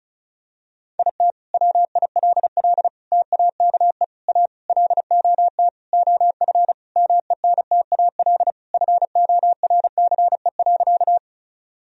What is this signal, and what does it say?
Morse code, heard as a keyed tone, about 36 wpm, 700 Hz. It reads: IT WILL TAKE A LOT OF MENTAL FORCE.
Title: Sentences from Top 1000 Words - Review Part 4 - 35wpm